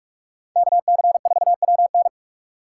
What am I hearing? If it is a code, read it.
Morse code, heard as a keyed tone, about 42 wpm, 700 Hz. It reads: KC4WN